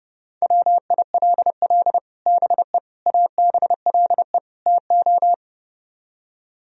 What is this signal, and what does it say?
Morse code, heard as a keyed tone, about 36 wpm, 700 Hz. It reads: WILL BE ABLE TO
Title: Speed-Racing - Sets of 4 Words 30wpm